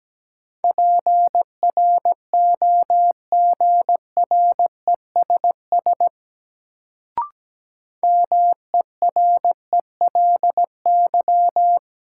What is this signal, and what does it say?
Morse code, heard as a keyed tone, about 17 wpm, 700 Hz. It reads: PROGRESS MERELY
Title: Sentences from Top 1400 Words - New Word Drill Alt - 17wpm